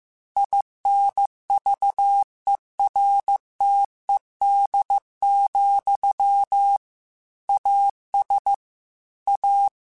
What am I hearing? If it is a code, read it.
Morse code, heard as a keyed tone, about 15 wpm, 800 Hz. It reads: INVERTED, AS A